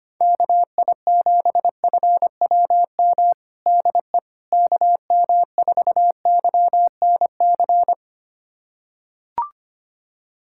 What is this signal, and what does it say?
Morse code, heard as a keyed tone, about 25 wpm, 700 Hz. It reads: KI7FWM DE KM4YNC
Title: USA Callsign DE Callsign - 6 Characters - 25wpm